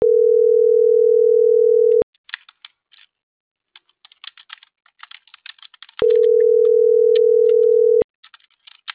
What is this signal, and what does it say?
Ringback tone (North America), 440 + 480 Hz.